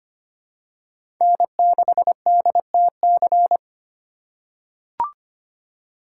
Morse code, heard as a keyed tone, about 25 wpm, 700 Hz. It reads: N6DTC
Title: USA Call Signs - 3x Character Spacing - 25wpm